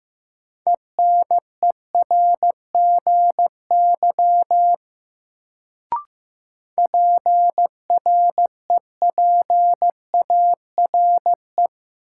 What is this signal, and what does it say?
Morse code, heard as a keyed tone, about 15 wpm, 700 Hz. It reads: ENERGY PREPARE
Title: Sentences from Top 900 Words - New Word Drill - 15wpm